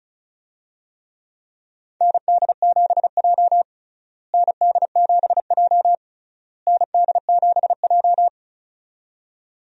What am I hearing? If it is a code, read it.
Morse code, heard as a keyed tone, about 36 wpm, 700 Hz. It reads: ND7J ND7J ND7J
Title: USA Call Signs - 4 Characters - Repeated 3x - 35wpm